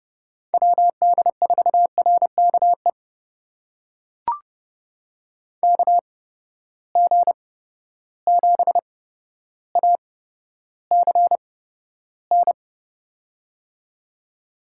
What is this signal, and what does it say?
Morse code, heard as a keyed tone, about 30 wpm, spaced as for 13 wpm, 700 Hz. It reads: WD4RKE KG7ACN
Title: USA Call Signs - 6 Characters - 8x Character Spacing - 30wpm